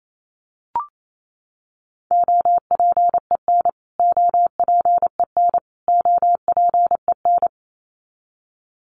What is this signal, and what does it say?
Morse code, heard as a keyed tone, about 28 wpm, 700 Hz. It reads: OPEN OPEN OPEN
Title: Top 500 Words Repeated 28wpm